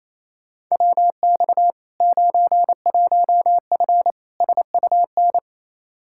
Morse code, heard as a keyed tone, about 27 wpm, 700 Hz. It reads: WX 91F SUN
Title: Practice QSO - Exchange 2 - Info - 28wpm